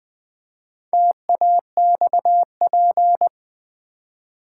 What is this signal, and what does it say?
Morse code, heard as a keyed tone, about 20 wpm, 700 Hz. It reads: TAXP
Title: Build A Word - 2 Letters At A Time - 8-Letter Words - 20wpm